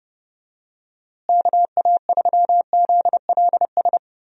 Morse code, heard as a keyed tone, about 30 wpm, 700 Hz. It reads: KA3ZLS
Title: USA Call Signs - 6 Characters - Repeated 3x - 30wpm